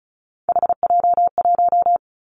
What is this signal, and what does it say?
Morse code, heard as a keyed tone, about 34 wpm, 700 Hz. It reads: HJ1
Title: Three Challenging Letter-Number Combinations 35wpm